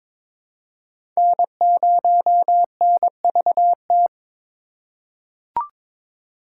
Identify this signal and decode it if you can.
Morse code, heard as a keyed tone, about 22 wpm, 700 Hz. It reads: N0NVT
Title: USA Call Signs - 5 Characters - 4x Character Spacing - 22wpm